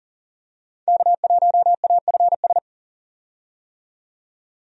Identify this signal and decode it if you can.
Morse code, heard as a keyed tone, about 40 wpm, 700 Hz. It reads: K1AFS
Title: USA Call Signs - 5 Characters - 40wpm